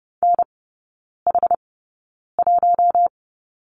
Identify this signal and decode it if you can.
Morse code, heard as a keyed tone, about 30 wpm, spaced as for 9 wpm, 700 Hz. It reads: NH1
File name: Lesson 16e - Letter c - Three Characters 30wpm